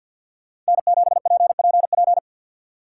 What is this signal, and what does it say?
Morse code, heard as a keyed tone, about 48 wpm, 700 Hz. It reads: N7PPP
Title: Speed Racing - USA 1x3 Call Signs 50wpm